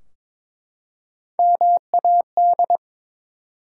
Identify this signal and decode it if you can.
Morse code, heard as a keyed tone, about 22 wpm, 700 Hz. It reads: MAD
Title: Three Letter Words - 22wpm